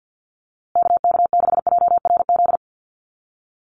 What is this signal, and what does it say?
Morse code, heard as a keyed tone, about 50 wpm, 700 Hz. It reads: KK6JRZ
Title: USA Call Signs Repeated 2x 50wpm